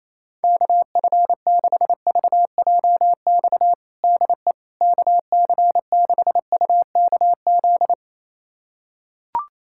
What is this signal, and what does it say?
Morse code, heard as a keyed tone, about 28 wpm, 700 Hz. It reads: KF6VJX DE KC6UKZ
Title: USA Callsign DE Callsign - 6 Characters - 28wpm